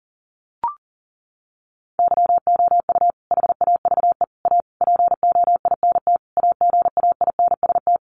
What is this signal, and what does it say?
Morse code, heard as a keyed tone, about 40 wpm, 700 Hz. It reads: YOU HAVE A POINT AGAINST
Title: Sentences from Top 1400 Words 40wpm